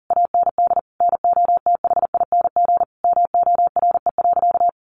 Morse code, heard as a keyed tone, about 40 wpm, 700 Hz. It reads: AND NOTHING MORE.
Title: Sentences from Top 800 Words 40wpm